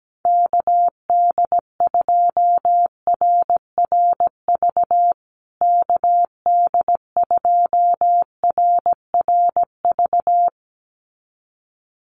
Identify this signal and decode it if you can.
Morse code, heard as a keyed tone, about 17 wpm, 700 Hz. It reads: KD2RRV KD2RRV
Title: USA Call Signs Repeated 2x 17wpm